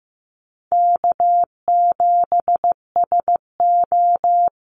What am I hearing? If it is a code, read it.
Morse code, heard as a keyed tone, about 15 wpm, 700 Hz. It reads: K7SO
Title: NA SOTA Activator Callsigns 15wpm